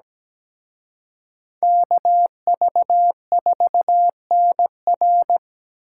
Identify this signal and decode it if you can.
Morse code, heard as a keyed tone, about 20 wpm, 700 Hz. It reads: KV4NR
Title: Speed Racing - USA 2x2 Call Signs 17wpm